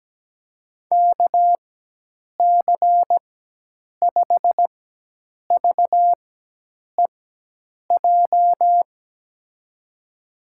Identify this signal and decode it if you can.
Morse code, heard as a keyed tone, about 17 wpm, spaced as for 8 wpm, 700 Hz. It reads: KC5VEJ
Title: USA Call Signs - 6 Characters - 4x Character Spacing - 17wpm